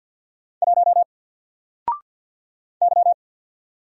Morse code, heard as a keyed tone, about 48 wpm, 700 Hz. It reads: 1 Y
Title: Single Letter-Number 50wpm